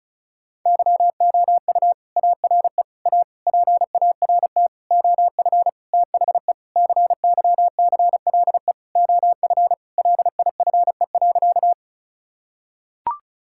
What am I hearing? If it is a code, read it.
Morse code, heard as a keyed tone, about 36 wpm, 690 Hz. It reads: YOU ARE A PART OF THE CYCLE OF LIFE.
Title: Sentences from Top 1000 Words - Review Part 6 - 35wpm